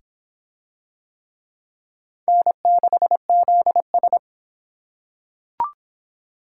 Morse code, heard as a keyed tone, about 23 wpm, 700 Hz. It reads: N6ZS